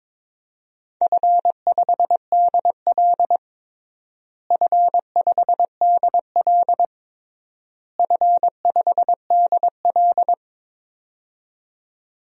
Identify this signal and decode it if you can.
Morse code, heard as a keyed tone, about 22 wpm, 700 Hz. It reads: F5DL F5DL F5DL
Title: DXCC-100 Call Signs - 4 Characters - Repeated 3x - 22wpm